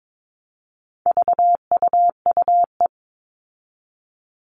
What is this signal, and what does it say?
Morse code, heard as a keyed tone, about 22 wpm, 700 Hz. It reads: VUUE
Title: All Cut Numbers - 4-Digit 22wpm